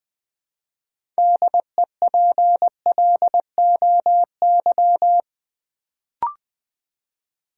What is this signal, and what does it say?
Morse code, heard as a keyed tone, about 20 wpm, 700 Hz. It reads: DEPLOY
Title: Build A Word - 2 Letters At A Time - 6-Letter Words - 20wpm